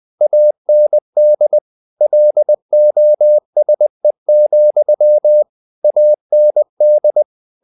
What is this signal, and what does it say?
Morse code, heard as a keyed tone, about 20 wpm, 600 Hz. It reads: AND LOSE, AND
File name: CW / Practice / If20